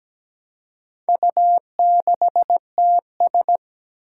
Morse code, heard as a keyed tone, about 17 wpm, 700 Hz. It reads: U6TS
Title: Four Challenging Letter-Number Combinations 17wpm